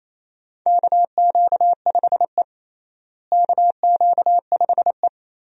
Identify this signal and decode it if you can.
Morse code, heard as a keyed tone, about 28 wpm, 700 Hz. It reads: KQ5E KQ5E
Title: USA Call Signs - 4 Characters - Repeated 3x - 28wpm